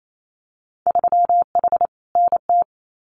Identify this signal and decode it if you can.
Morse code, heard as a keyed tone, about 28 wpm, 700 Hz. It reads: 3H NT
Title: Winter Field Day Exchanges Encore 28wpm